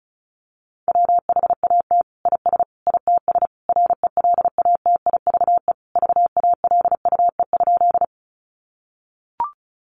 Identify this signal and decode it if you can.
Morse code, heard as a keyed tone, about 34 wpm, 700 Hz. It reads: WHAT IS ITS RELATIVE VALUE?
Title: Sentences from Top 1400 Words 35wpm